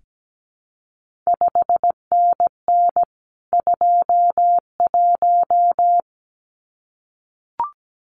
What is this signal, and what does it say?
Morse code, heard as a keyed tone, about 17 wpm, 700 Hz. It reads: 5NN 21
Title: CQ WW DX Contest Exchange Encore - 17wpm